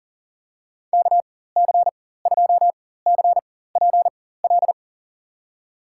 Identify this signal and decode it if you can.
Morse code, heard as a keyed tone, about 37 wpm, spaced as for 18 wpm, 700 Hz. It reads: KC2CPL
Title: USA Call Signs - 4x Character Spacing - 40wpm